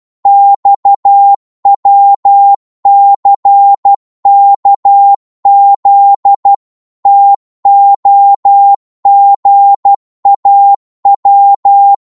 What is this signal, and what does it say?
Morse code, heard as a keyed tone, about 12 wpm, 800 Hz. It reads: XWCKZ TOGAW
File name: Morse / ltr-12wpm